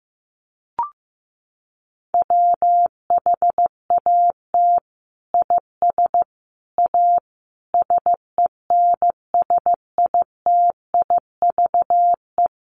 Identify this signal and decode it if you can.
Morse code, heard as a keyed tone, about 15 wpm, 700 Hz. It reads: WHAT IS A SENSITIVE
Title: Sentences from Top 1400 Words 15wpm